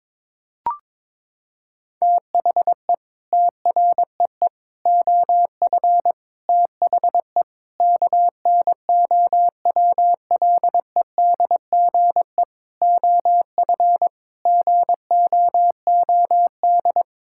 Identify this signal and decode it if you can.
Morse code, heard as a keyed tone, about 22 wpm, 700 Hz. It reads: THE TREE OF THE KNOWLEDGE OF GOOD
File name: Sentences with 9-Word Phrases - 22wpm